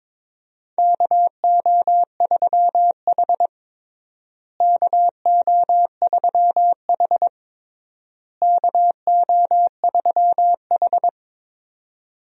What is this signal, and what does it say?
Morse code, heard as a keyed tone, about 22 wpm, 700 Hz. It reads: KO3H KO3H KO3H
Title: USA Call Signs - 4 Characters - Repeated 3x - 22wpm